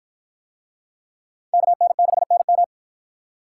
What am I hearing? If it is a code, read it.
Morse code, heard as a keyed tone, about 50 wpm, 700 Hz. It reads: KN6NM